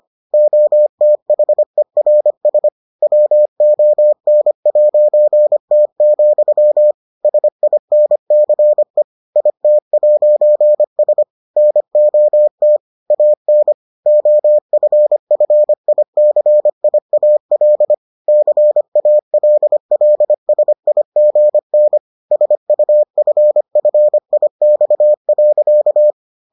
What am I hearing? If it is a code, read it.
Morse code, heard as a keyed tone, about 25 wpm, 600 Hz. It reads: OTHERS WON'T, SINCE IT'S NOT AN OFFICIAL CALLSIGN SUFFIX.